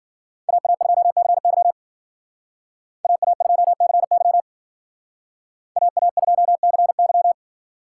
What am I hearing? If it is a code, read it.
Morse code, heard as a keyed tone, about 52 wpm, 700 Hz. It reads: AA2CY AA2CY AA2CY